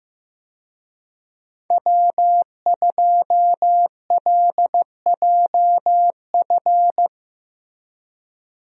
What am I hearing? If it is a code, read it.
Morse code, heard as a keyed tone, about 15 wpm, 700 Hz. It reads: W2LJF
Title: USA Call Signs - 5 Characters - Repeated 2x - 15wpm